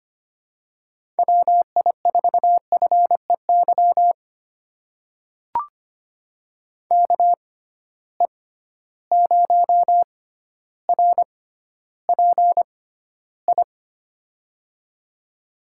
Morse code, heard as a keyed tone, about 25 wpm, spaced as for 13 wpm, 700 Hz. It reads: WI4FEY KE0RPI
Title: USA Call Signs - 6 Characters - 6x Character Spacing - 25wpm